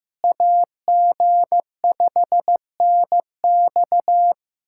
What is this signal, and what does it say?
Morse code, heard as a keyed tone, about 14 wpm, 700 Hz. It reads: AG5NX
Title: USA Callsign DE Callsign - 5 Characters - 15wpm